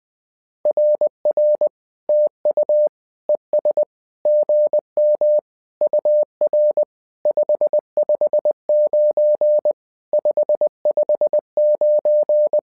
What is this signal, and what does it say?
Morse code, heard as a keyed tone, about 20 wpm, 600 Hz. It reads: RR TU ES GM UR 559 559
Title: SOTA QSO Practice 20wpm